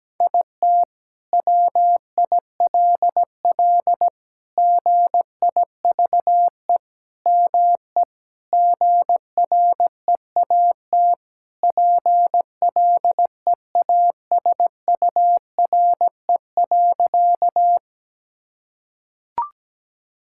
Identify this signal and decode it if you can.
Morse code, heard as a keyed tone, about 17 wpm, 700 Hz. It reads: IT WILL GIVE ME GREAT PLEASURE.
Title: Sentences from Top 1600 Words 17wpm